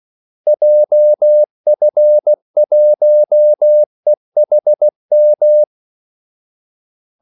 Morse code, heard as a keyed tone, about 16 wpm, 600 Hz. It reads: JF1EHM
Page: Kaldesignaler 16 wpm